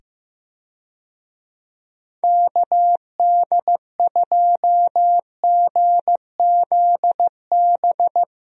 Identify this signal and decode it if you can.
Morse code, heard as a keyed tone, about 15 wpm, 700 Hz. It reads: KD2GZB
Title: USA Call Signs - 6 Characters - 15wpm